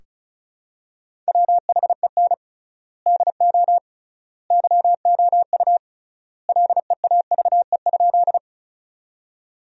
Morse code, heard as a keyed tone, about 36 wpm, 700 Hz. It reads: WHEN DO YOU LEAVE?
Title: 3x Word-Spacing - Sentences from Top 400 Words - 35wpm